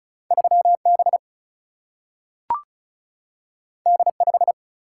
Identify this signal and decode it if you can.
Morse code, heard as a keyed tone, about 36 wpm, 700 Hz. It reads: 3B D5
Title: Single-Letter Single-Number 35wpm